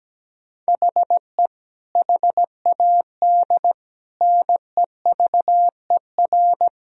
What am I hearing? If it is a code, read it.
Morse code, heard as a keyed tone, about 17 wpm, 700 Hz. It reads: HE HAD NEVER
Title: Sentences from Top 1000 Words - Review Part 3 - 17wpm